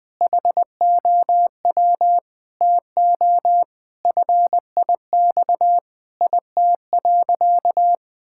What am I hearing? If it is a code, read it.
Morse code, heard as a keyed tone, about 20 wpm, 700 Hz. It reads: HOW TO FIX IT.
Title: Sentences with 6-Word Phrases - 20wpm